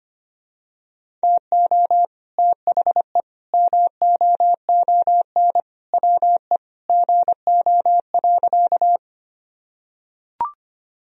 Morse code, heard as a keyed tone, about 25 wpm, 700 Hz. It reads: TO THE MOON WE GO.
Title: Sentences from Top 1000 Words - Review Part 3 - 25wpm